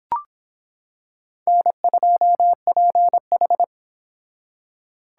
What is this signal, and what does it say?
Morse code, heard as a keyed tone, about 25 wpm, 700 Hz. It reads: N2PH